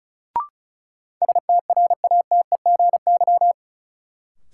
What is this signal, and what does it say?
Morse code, heard as a keyed tone, about 36 wpm, 700 Hz. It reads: STRATEGY
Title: Sentences from Top 1000 Words - Review Part 5 - New Word Drill - 35wpm